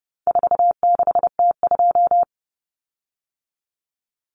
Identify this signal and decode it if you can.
Morse code, heard as a keyed tone, about 30 wpm, 700 Hz. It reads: 46T2
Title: Common Cut Numbers - 4-Digit 30wpm